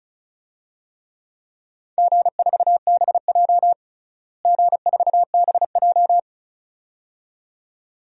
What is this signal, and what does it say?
Morse code, heard as a keyed tone, about 36 wpm, 700 Hz. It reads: G4BJ G4BJ
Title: DXCC-100 Call Signs - 4 Characters - Repeated 2x - 35wpm